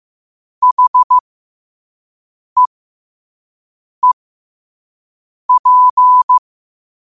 Morse code, heard as a keyed tone, about 15 wpm, spaced as for 5 wpm, 1000 Hz. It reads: HEEP
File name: qotdfiles / 5wpm